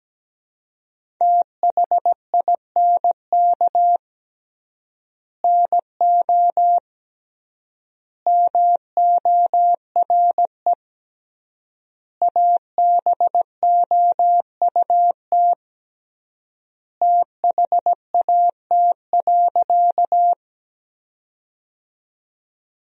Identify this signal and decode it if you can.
Morse code, heard as a keyed tone, about 17 wpm, 700 Hz. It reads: THINK NO MORE ABOUT THAT.
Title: 3x Word-Spacing - Sentences from Top 200 Words - 17wpm